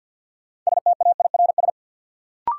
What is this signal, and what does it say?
Morse code, heard as a keyed tone, about 47 wpm, 700 Hz. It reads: STAIRS
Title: Sets of 5 Words Encore - New Word Drill - 50wpm